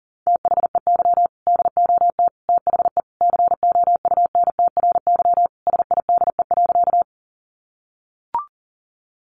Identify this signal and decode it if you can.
Morse code, heard as a keyed tone, about 37 wpm, 700 Hz. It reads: THEY DOT THE COUNTRYSIDE.
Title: Sentences from Top 1500 Words 40wpm